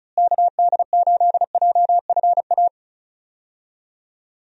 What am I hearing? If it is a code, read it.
Morse code, heard as a keyed tone, about 34 wpm, 700 Hz. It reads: KD8JFA
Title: USA Call Signs - 6 Characters - Repeated 3x - 35wpm